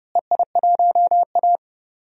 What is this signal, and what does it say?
Morse code, heard as a keyed tone, about 32 wpm, 700 Hz. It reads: EI1A
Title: DXCC-100 Call Signs - 4 Characters - Repeated 3x - 30wpm